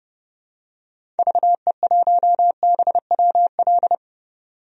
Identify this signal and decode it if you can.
Morse code, heard as a keyed tone, about 32 wpm, 700 Hz. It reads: VE1BWL
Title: DXCC-100 Call Signs - 6 Characters - 30wpm